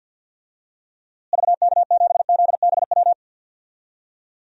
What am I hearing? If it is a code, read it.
Morse code, heard as a keyed tone, about 53 wpm, 700 Hz. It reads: VK7ZBW
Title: DXCC-100 Call Signs - 6 Characters - Repeated 2x - 50wpm